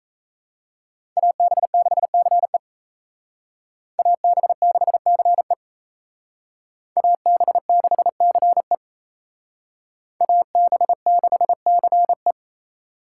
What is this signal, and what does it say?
Morse code, heard as a keyed tone, about 37 wpm, 700 Hz. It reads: AB6CE AB6CE AB6CE AB6CE